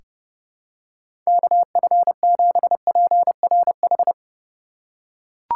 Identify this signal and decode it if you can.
Morse code, heard as a keyed tone, about 30 wpm, 700 Hz. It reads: KF7PRH
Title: USA Call Signs - 4x Character Spacing - 30wpm